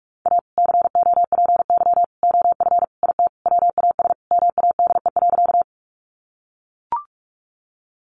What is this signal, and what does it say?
Morse code, heard as a keyed tone, about 44 wpm, 700 Hz. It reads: A COPY OF IT WAS MADE.
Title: Sentences from Top 1400 Words 45wpm